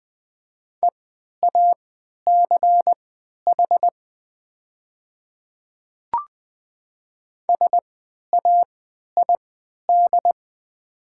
Morse code, heard as a keyed tone, about 20 wpm, spaced as for 11 wpm, 700 Hz. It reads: EACH SAID
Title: 3x Character Spacing - Top 200 Words - 20wpm